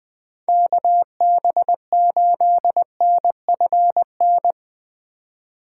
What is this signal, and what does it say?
Morse code, heard as a keyed tone, about 23 wpm, 700 Hz. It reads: KB8NFN
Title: Speed Racing - USA 2x3 Call Signs 20wpm